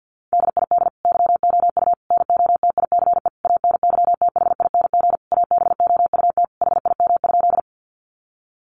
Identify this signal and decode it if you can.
Morse code, heard as a keyed tone, about 50 wpm, 700 Hz. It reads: DID YOU NOTICE ANYTHING ABOUT HIM?
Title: Sentences from Top 1000 Words 50wpm